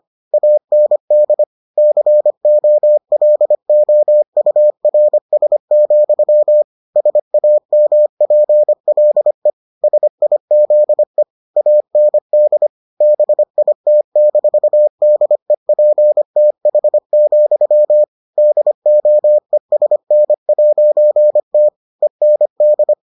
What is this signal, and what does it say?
Morse code, heard as a keyed tone, about 25 wpm, 600 Hz. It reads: AND COLOURS, SAMPLE SIZE AND BIT-DEPTH, DOESN'T END